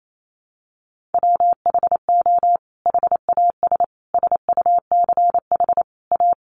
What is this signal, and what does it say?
Morse code, heard as a keyed tone, about 28 wpm, 700 Hz. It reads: WHO HAS SUCH A
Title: Sentences from Top 1200 Words 28wpm